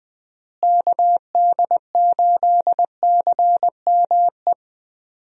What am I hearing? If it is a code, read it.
Morse code, heard as a keyed tone, about 20 wpm, 700 Hz. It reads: KD8CME
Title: USA Call Signs - 6 Characters - Repeated 3x - 20wpm